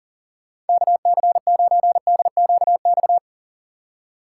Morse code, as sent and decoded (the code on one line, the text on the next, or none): -.- -.-. ----. -.. --.- -..-
KC9DQX